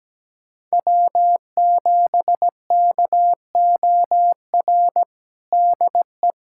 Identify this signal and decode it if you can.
Morse code, heard as a keyed tone, about 17 wpm, 700 Hz. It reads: W7KOR DE
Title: USA Callsign DE Callsign 17wpm